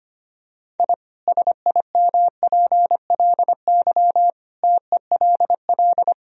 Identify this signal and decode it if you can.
Morse code, heard as a keyed tone, about 25 wpm, 700 Hz. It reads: I SIMPLY TELL
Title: Sentences from Top 1000 Words - Review Part 4 - 25wpm